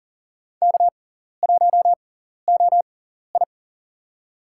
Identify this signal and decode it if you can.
Morse code, heard as a keyed tone, about 40 wpm, spaced as for 14 wpm, 700 Hz. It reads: K1OI
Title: USA Call Signs - 4 Characters - 6x Character Spacing - 40wpm